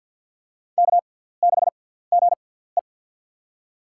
Morse code, heard as a keyed tone, about 53 wpm, spaced as for 17 wpm, 690 Hz. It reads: K6GE
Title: USA Call Signs - 4 Characters - 6x Character Spacing - 50wpm